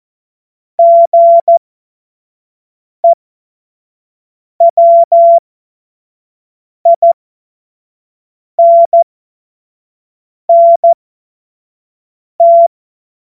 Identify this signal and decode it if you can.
Morse code, heard as a keyed tone, about 14 wpm, spaced as for 5 wpm, 680 Hz. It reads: GEWINNT
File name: Morsekurs / Witze / Witz 21. Juni 2022 25 F70